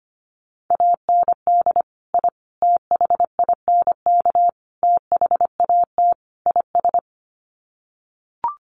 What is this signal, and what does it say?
Morse code, heard as a keyed tone, about 25 wpm, 700 Hz. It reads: AND I THINK THAT IS
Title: Sets of 5 Words Encore2 25wpm